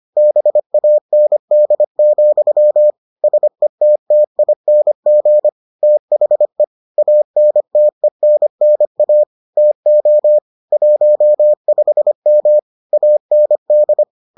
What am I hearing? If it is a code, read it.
Morse code, heard as a keyed tone, about 25 wpm, 600 Hz. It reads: BAND, SETTING THE ANTENNA TO 15M AND